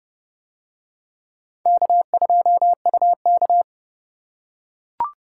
Morse code, heard as a keyed tone, about 29 wpm, 700 Hz. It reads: K2UK